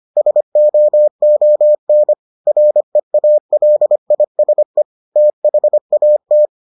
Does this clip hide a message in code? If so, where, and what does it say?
Yes: Morse code from 0.2 to 6.5 s, heard as a keyed tone, about 25 wpm, 600 Hz. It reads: SOON REALISE THAT